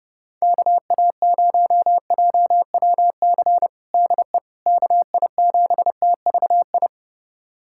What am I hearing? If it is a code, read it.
Morse code, heard as a keyed tone, about 32 wpm, 700 Hz. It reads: KA0JWC DE KI7TVI
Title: USA Callsign DE Callsign - 6 Characters - 30wpm